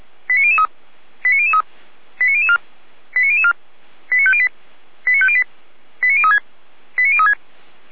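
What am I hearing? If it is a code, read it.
Five-tone selective call (ZVEI), heard as quick runs of five tones, tones of about 70 ms. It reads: 89003 89003 89004 89004 89508 89508 89036 89036
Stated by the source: ZVEI five-tone call received on an SDR, feuerwehr-orlamuende.org fire brigade audio server.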